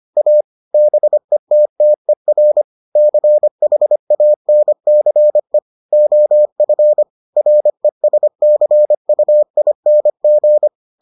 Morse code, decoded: A BETTER CHANCE OF RESCUING

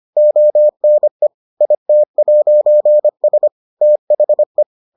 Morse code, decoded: ONE IT'S THE

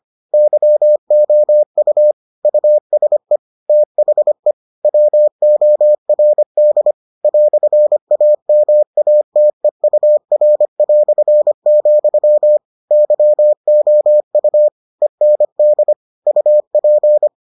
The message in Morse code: YOU USE THE WORD "AMATEUR", YOU END UP

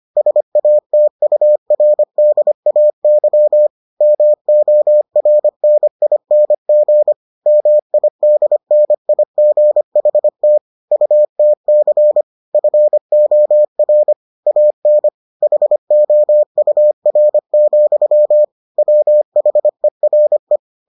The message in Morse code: SATURDAY MORNING MIDNIGHT UTC FOR AN HOUR, WHERE

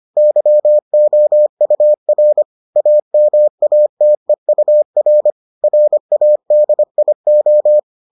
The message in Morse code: YOUR AMATEUR RADIO

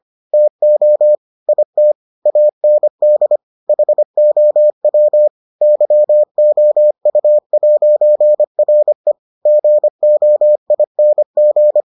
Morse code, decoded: TO IT AND HOW YOU'RE GOING

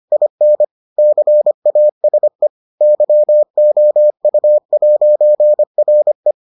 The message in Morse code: IN CASE YOU'RE

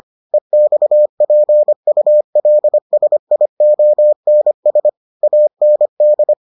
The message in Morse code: EXPULSIONS AND

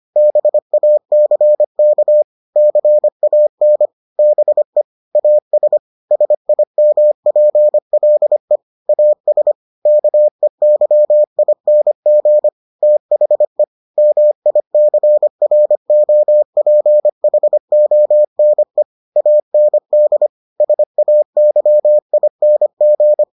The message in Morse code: BACK CAN BE AS SIMPLE AS KEYING THE MICROPHONE AND SAYING

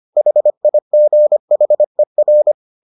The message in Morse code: HIGHER

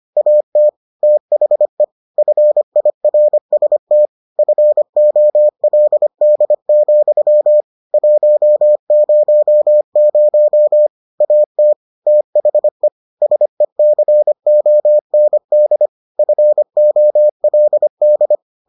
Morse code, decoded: AT THE FIRST FOLD, 100 AT THE SECOND FOLD

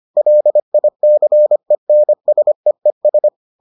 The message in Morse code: LICENSEES

